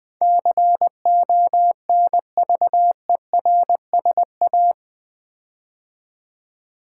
Morse code, heard as a keyed tone, about 21 wpm, 700 Hz. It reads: CONVERSA
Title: Build A Word - 4 Letters At A Time - 12-Letter Words - 20wpm